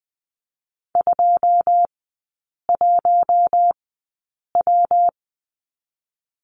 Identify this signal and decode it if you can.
Morse code, heard as a keyed tone, about 20 wpm, 700 Hz. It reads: 2 1 W
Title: 114 Lesson 19e - Letter w - Three Characters 20wpm